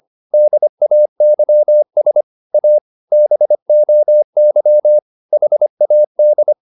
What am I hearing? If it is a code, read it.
Morse code, heard as a keyed tone, about 25 wpm, 600 Hz. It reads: DAYS A BOY HAD